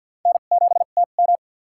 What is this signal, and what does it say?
Morse code, heard as a keyed tone, about 48 wpm, spaced as for 35 wpm, 700 Hz. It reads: N7TM
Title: USA Call Signs - 4 Characters - 2x Character Spacing - 50wpm